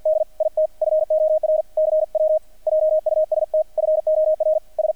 Morse code, heard as a keyed tone, about 50 wpm, 610 Hz. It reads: D IT FOR CW JUST FOR S
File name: archived / CRSnet